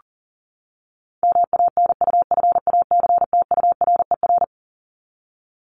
Morse code, heard as a keyed tone, about 40 wpm, 700 Hz. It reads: MANUFACTURER
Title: Top 1751-2000 Words 40wpm